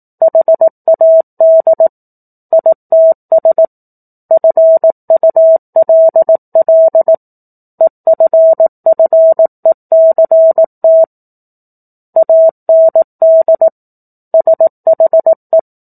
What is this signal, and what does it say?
Morse code, heard as a keyed tone, about 18 wpm, 650 Hz. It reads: HAD ITS FULL EFFECT AND SHE
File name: 04-Chapter-04 18 WPM